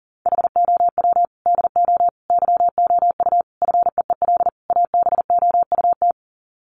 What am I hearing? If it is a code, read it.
Morse code, heard as a keyed tone, about 40 wpm, 700 Hz. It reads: HOW DO YOU FEEL ABOUT
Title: Sets of 5 Words 40wpm